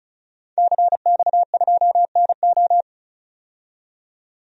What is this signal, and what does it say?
Morse code, heard as a keyed tone, about 34 wpm, 700 Hz. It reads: CX2NO
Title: DXCC-100 Call Signs - 5 Characters - 35wpm